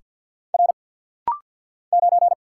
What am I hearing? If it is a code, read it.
Morse code, heard as a keyed tone, about 50 wpm, 700 Hz. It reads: R 9
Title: Single Letter-Number - Rapid-Fire 50wpm